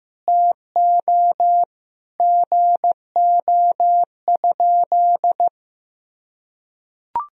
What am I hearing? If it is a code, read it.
Morse code, heard as a keyed tone, about 15 wpm, 700 Hz. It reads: TO GO?
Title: Sentences from Top 200 Words and Abbreviations 15wpm